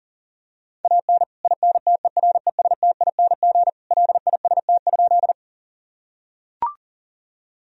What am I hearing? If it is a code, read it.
Morse code, heard as a keyed tone, about 40 wpm, 700 Hz. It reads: AN INTERESTING LIST?
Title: Sentences from Top 1000 Words - Review Part 2 - 40wpm